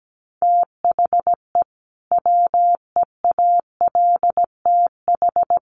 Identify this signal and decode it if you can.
Morse code, heard as a keyed tone, about 17 wpm, 700 Hz. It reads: THE WEALTH
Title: Sentences from Top 1200 Words 17wpm